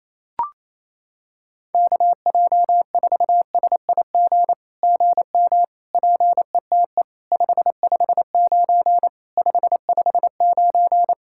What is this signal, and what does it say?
Morse code, heard as a keyed tone, about 28 wpm, 700 Hz. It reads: KJ4SIG GM PETE 559 559
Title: SOTA QSO Practice 28wpm